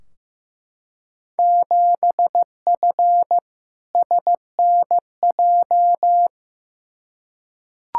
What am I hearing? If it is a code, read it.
Morse code, heard as a keyed tone, about 15 wpm, 700 Hz. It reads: 7F SNJ
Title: Field Day Exchanges Encore - 15wpm